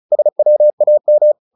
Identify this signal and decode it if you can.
Morse code, heard as a keyed tone, about 34 wpm, 600 Hz. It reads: SWAM
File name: the old man and the sea / the old man and the sea 1 35wpm